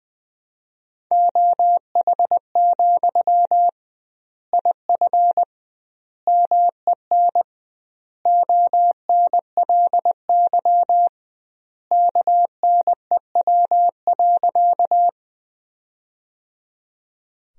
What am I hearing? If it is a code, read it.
Morse code, heard as a keyed tone, about 20 wpm, 700 Hz. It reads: OH, IF MEN ONLY KNEW.
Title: 2x Word-Spacing - Sentences from Top 500 Words - 20wpm